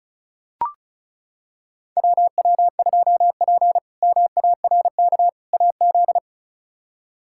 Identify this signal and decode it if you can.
Morse code, heard as a keyed tone, about 36 wpm, 700 Hz. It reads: WW2P MARK AZ
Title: K1USN SST QSO Practice - 35wpm